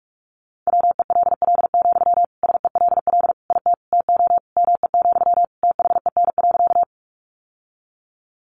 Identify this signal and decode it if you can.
Morse code, heard as a keyed tone, about 44 wpm, 700 Hz. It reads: WELL, SELL IT TO ME, THEN.
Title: Sentences from Top 1100 Words 45wpm